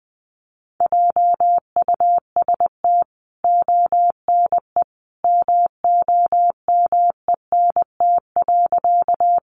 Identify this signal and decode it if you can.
Morse code, heard as a keyed tone, about 20 wpm, 700 Hz. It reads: JUST ONE MOMENT.
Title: Sentences from Top 1000 Words 20wpm